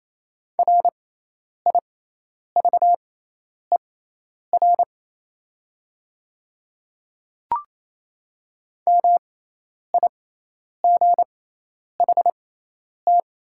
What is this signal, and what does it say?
Morse code, heard as a keyed tone, about 28 wpm, spaced as for 10 wpm, 700 Hz. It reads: RIVER MIGHT
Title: 6x Character Spacing - Top 300 Words - 28wpm